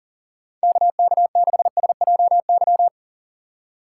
Morse code, heard as a keyed tone, about 40 wpm, 700 Hz. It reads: KK6SJY
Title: USA Call Signs - 6 Characters - 2x Character Spacing - 40wpm